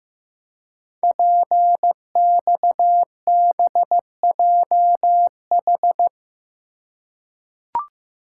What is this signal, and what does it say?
Morse code, heard as a keyed tone, about 15 wpm, 700 Hz. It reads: PXBJH
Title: Code Groups - 5 Letters - 15wpm